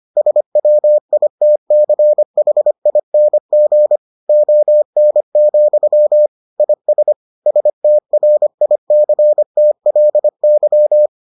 Morse code, heard as a keyed tone, about 25 wpm, 600 Hz. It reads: SWITCHING ON, IS STRICTLY